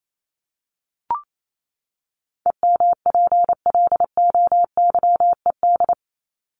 Morse code, heard as a keyed tone, about 28 wpm, 700 Hz. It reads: EMPLOYED
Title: ED Suffix - Top 100 Words with Base Words 28wpm